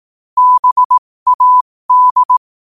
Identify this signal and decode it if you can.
Morse code, heard as a keyed tone, about 18 wpm, spaced as for 15 wpm, 1000 Hz. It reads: BAD
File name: qotdfiles / 15wpm